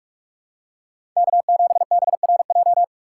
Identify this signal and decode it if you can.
Morse code, heard as a keyed tone, about 46 wpm, 690 Hz. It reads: K7BRJ
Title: USA Call Signs - 5 Characters - 4x Character Spacing - 45wpm